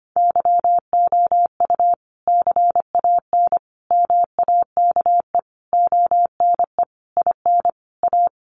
Morse code, heard as a keyed tone, about 25 wpm, 700 Hz. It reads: YOU CAN MAKE ONE IN A
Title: Sentences from Top 1000 Words 25wpm